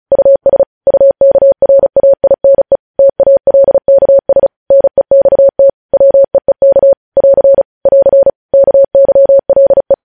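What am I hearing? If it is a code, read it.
Morse code, heard as a keyed tone, about 36 wpm, 550 Hz. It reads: US UKRAINE TALKS NEXT WEEK <AR> <AR> KYLE
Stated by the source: News headlines sent in Morse code at 35 WPM.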